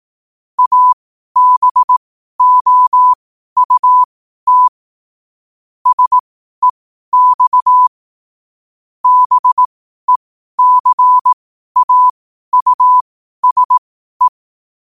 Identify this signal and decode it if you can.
Morse code, heard as a keyed tone, about 18 wpm, spaced as for 12 wpm, 1000 Hz. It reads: ABOUT SEX BECAUSE
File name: qotdfiles / 12wpm